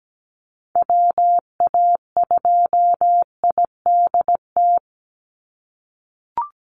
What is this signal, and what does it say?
Morse code, heard as a keyed tone, about 17 wpm, 700 Hz. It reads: WA2IDT
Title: USA Call Signs 17wpm